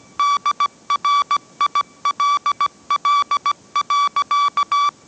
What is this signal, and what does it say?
Morse code, heard as a keyed tone, about 17 wpm, 1150 Hz. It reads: DRILL.
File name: mrx / mrx080 18wpm